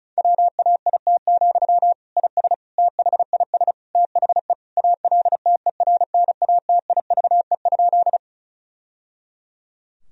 Morse code, heard as a keyed tone, about 34 wpm, 700 Hz. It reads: WAIT, IS THIS THE ALTERNATIVE?
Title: Sentences from Top 1000 Words - Review Part 4 - 35wpm